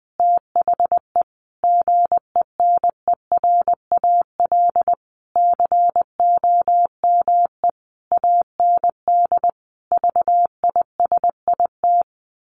Morse code, heard as a keyed tone, about 19 wpm, 700 Hz. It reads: THE GENERAL COME AND VISIT
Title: Sentences from Top 1400 Words 20wpm